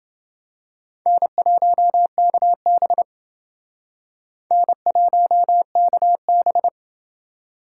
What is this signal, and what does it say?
Morse code, heard as a keyed tone, about 28 wpm, 700 Hz. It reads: N1KB N1KB